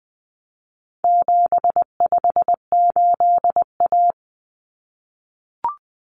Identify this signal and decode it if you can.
Morse code, heard as a keyed tone, about 20 wpm, 700 Hz. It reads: 758A
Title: Common Cut Numbers - 4-Digit 20wpm